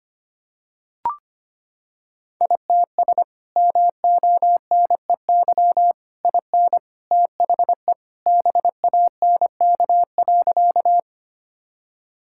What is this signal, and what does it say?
Morse code, heard as a keyed tone, about 25 wpm, 700 Hz. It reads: ITS MONEY IN THE BANK.
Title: Sentences from Top 1000 Words - Review Part 6 - 25wpm